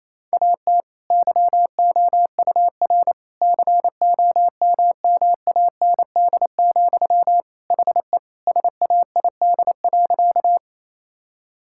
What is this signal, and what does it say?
Morse code, heard as a keyed tone, about 28 wpm, 700 Hz. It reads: AT YOUR COMMAND, HE SAID.
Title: Sentences from Top 1600 Words 28wpm